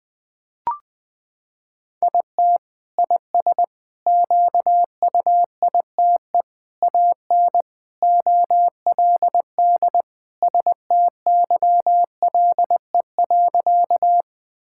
Morse code, heard as a keyed tone, about 21 wpm, 700 Hz. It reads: IT IS QUITE AN OLD STYLE.
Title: Sentences from Top 1000 Words - Review Part 5 - 20wpm